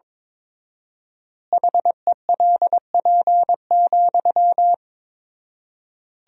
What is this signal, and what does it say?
Morse code, heard as a keyed tone, about 22 wpm, 700 Hz. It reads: HELP,
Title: 4x Word-Spacing - Sentences from Top 300 Words - 22wpm